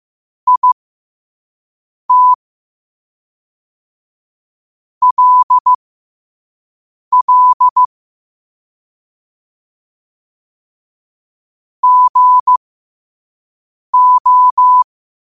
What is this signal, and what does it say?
Morse code, heard as a keyed tone, about 15 wpm, spaced as for 5 wpm, 1000 Hz. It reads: IT LL GO
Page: New quotes every day in morse code at 5 Words per minute.